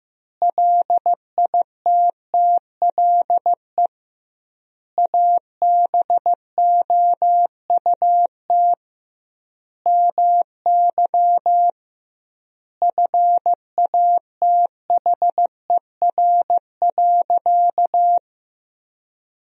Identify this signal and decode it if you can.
Morse code, heard as a keyed tone, about 16 wpm, 700 Hz. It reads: LITTLE ABOUT MY FATHER.
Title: 2x Word-Spacing - Sentences from Top 400 Words - 15wpm